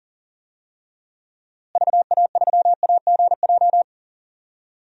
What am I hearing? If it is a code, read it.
Morse code, heard as a keyed tone, about 40 wpm, 700 Hz. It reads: VA3AGJ
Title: DXCC-100 Call Signs - 6 Characters - Repeated 3x - 40wpm